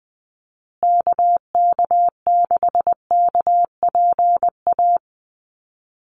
Morse code, heard as a keyed tone, about 20 wpm, 700 Hz. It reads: KK6KPA
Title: USA Call Signs Repeated 3x 20wpm